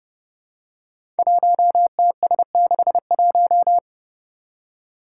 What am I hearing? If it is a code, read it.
Morse code, heard as a keyed tone, about 30 wpm, 700 Hz. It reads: 1TS61
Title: Five Challenging Letter-Number Combinations 30wpm